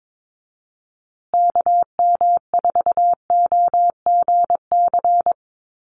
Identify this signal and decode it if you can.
Morse code, heard as a keyed tone, about 22 wpm, 700 Hz. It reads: KM4OGC
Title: USA Call Signs Repeated 3x 22wpm